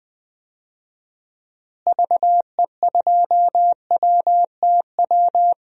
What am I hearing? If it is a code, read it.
Morse code, heard as a keyed tone, about 20 wpm, 700 Hz. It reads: VE2WTW
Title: DXCC-100 Call Signs - 6 Characters - Repeated 2x - 20wpm